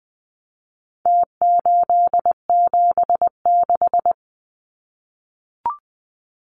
Morse code, heard as a keyed tone, about 20 wpm, 700 Hz. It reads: T876
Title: Numbers - Contest Style - 4-Digit 20wpm